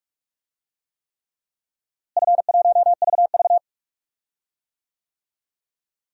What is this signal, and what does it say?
Morse code, heard as a keyed tone, about 46 wpm, 700 Hz. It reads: F1VV
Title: DXCC-100 Call Signs - 4 Characters - Repeated 3x - 45wpm